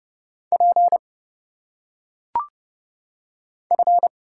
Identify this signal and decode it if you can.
Morse code, heard as a keyed tone, about 30 wpm, 700 Hz. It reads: P F
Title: Single Letters - Phonetic Pronunciation 30wpm